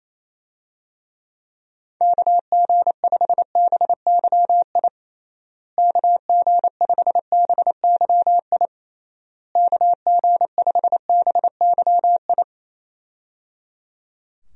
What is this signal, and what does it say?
Morse code, heard as a keyed tone, about 28 wpm, 700 Hz. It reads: KG5BYI KG5BYI KG5BYI
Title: USA Call Signs - 6 Characters - Repeated 3x - 28wpm